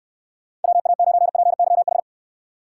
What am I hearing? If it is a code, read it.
Morse code, heard as a keyed tone, about 64 wpm, 700 Hz. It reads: WA9PYH